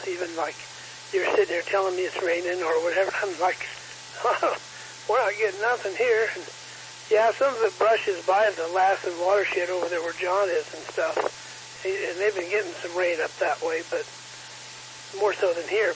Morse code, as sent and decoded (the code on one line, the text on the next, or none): none